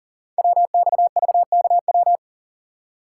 Morse code, heard as a keyed tone, about 40 wpm, 700 Hz. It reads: WXVKW
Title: Code Groups - 5 Letters - 40wpm